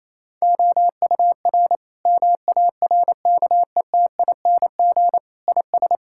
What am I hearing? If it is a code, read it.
Morse code, heard as a keyed tone, about 28 wpm, 700 Hz. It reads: OUR MARKETING IS
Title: Sentences from Top 1000 Words - Review Part 5 - 28wpm